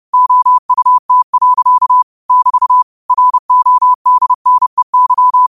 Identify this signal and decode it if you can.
Morse code, heard as a keyed tone, about 30 wpm, 1000 Hz. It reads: OUT. <BT> RODNEY